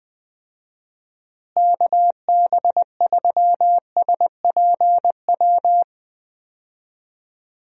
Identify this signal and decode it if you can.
Morse code, heard as a keyed tone, about 20 wpm, 700 Hz. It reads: KB3SPW
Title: USA Call Signs - 6 Characters - Repeated 3x - 20wpm